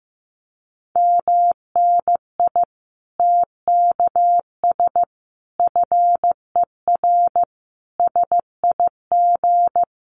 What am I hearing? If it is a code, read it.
Morse code, heard as a keyed tone, about 15 wpm, 700 Hz. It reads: MNI TKS FER SIG
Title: Common QSO Phrases 15wpm